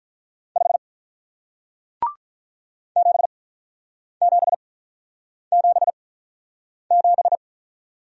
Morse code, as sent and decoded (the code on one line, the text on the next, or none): ..... / --... --... --... --...
5 7777